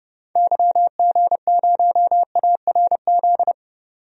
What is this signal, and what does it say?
Morse code, heard as a keyed tone, about 31 wpm, 700 Hz. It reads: YG0ARZ
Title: DXCC-100 Call Signs - 6 Characters - 30wpm